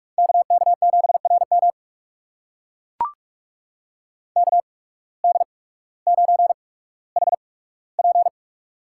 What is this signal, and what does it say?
Morse code, heard as a keyed tone, about 44 wpm, spaced as for 19 wpm, 700 Hz. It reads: KK7RM KD9HP
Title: USA Call Signs - 8x Character Spacing - 45wpm